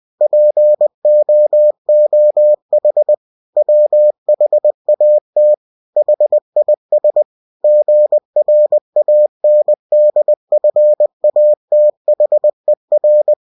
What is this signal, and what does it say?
Morse code, heard as a keyed tone, about 20 wpm, 600 Hz. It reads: POOH WHAT HIS GRANDFATHER
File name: cw-books / winnie-the-pooh / winnie-20wpm-page-0022